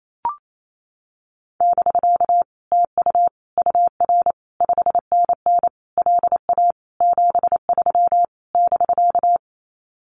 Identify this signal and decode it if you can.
Morse code, heard as a keyed tone, about 28 wpm, 700 Hz. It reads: <BK> TU UR 5NN LA 73 <BK>
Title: Common QSO Phrases Encore 28wpm